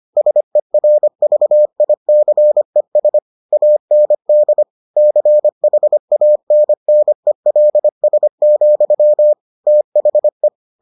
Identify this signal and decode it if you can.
Morse code, heard as a keyed tone, about 25 wpm, 600 Hz. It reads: SERVICES AND CHANNELS, THE